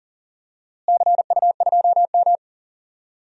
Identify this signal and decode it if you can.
Morse code, heard as a keyed tone, about 40 wpm, 700 Hz. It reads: CU2M
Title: DXCC-100 Call Signs - 4 Characters - Repeated 2x - 40wpm